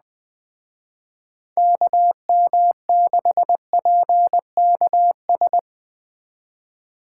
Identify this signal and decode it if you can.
Morse code, heard as a keyed tone, about 20 wpm, 700 Hz. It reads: KM6PKS
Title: Speed Racing - USA Call Signs 17wpm